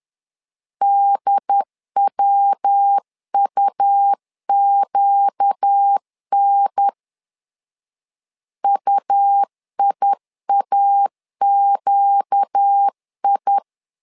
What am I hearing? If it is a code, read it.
Morse code, heard as a keyed tone, about 10 wpm, 800 Hz. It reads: DWUQN UIAQI
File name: LEZIONI / 4 Q 10wpm 2210